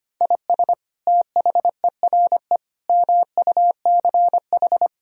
Morse code, heard as a keyed tone, about 25 wpm, 700 Hz. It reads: IS THERE MUCH
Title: Sentences from Top 1000 Words - Review Part 4 - 25wpm